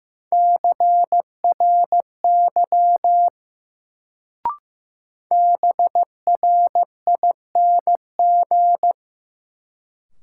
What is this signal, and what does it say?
Morse code, heard as a keyed tone, about 15 wpm, 700 Hz. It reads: CRY BRING
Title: Sets of 5 Words Encore - New Word Drill - 15wpm